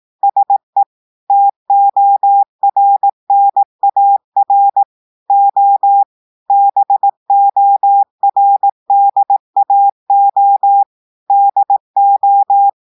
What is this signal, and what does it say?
Morse code, heard as a keyed tone, about 18 wpm, 800 Hz. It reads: SE TORNAR O BORDAO DO
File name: audios / portugais / portugais texte en clair 2 vitesse 18 mots minutes